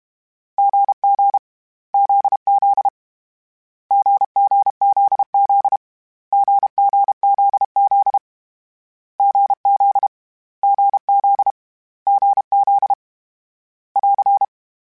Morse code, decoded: GG ZZ GGZZ GGZZ GZ GZ GZ <AR>

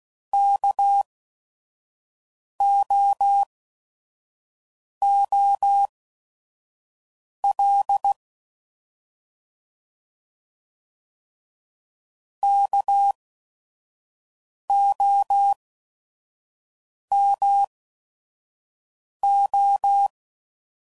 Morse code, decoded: KOOL KOMO

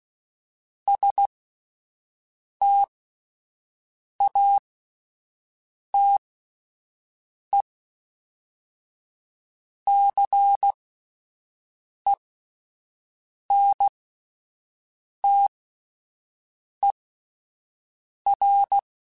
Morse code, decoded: STATE CENTER

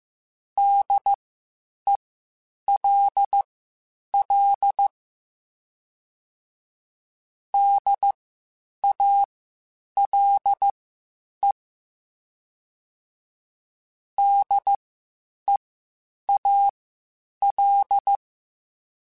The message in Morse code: DELL DALE DEAL

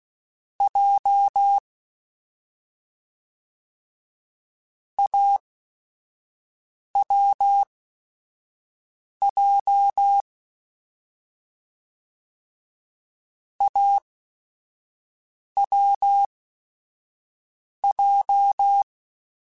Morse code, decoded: J AWJ AWJ

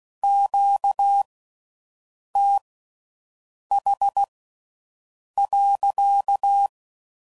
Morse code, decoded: QTH.

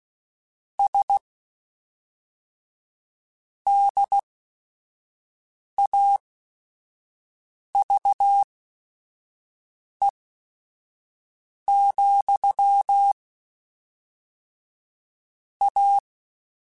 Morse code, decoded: S DAVE, A

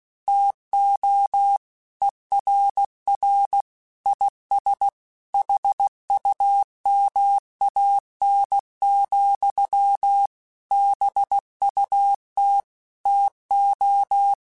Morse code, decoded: TO ERR IS HUMAN, BUT TO

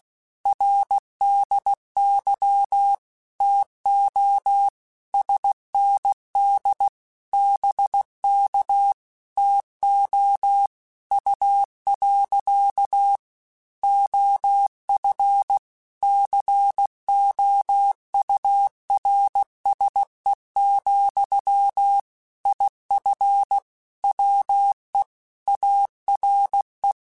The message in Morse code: RDY TO SND BK TO U. OF COURSE, IF WE ARE

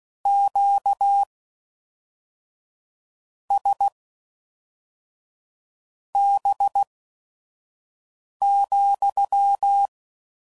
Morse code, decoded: QSB,

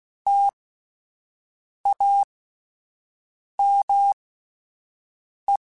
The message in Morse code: TAME

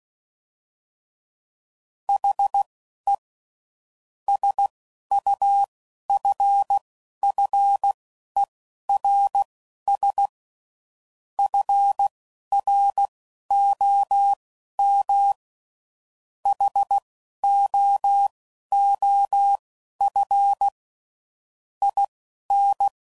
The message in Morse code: HE SUFFERS FROM HOOF IN

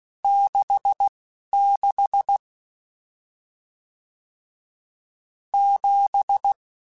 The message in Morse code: 66 7

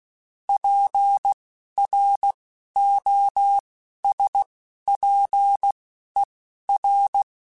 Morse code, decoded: PROSPER

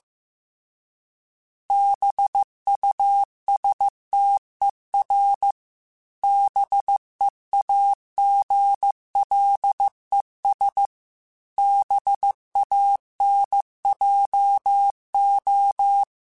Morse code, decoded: BUSTER BEAGLES BANJO

